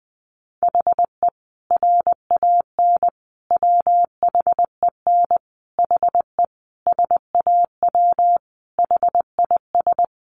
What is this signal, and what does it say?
Morse code, heard as a keyed tone, about 20 wpm, 700 Hz. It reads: HE RAN WHEN HE SAW HIS
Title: Sentences from Top 500 Words 20wpm